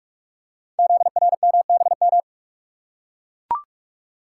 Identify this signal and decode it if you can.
Morse code, heard as a keyed tone, about 44 wpm, 690 Hz. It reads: ZRMBM
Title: Code Groups - 5 Letters - 45wpm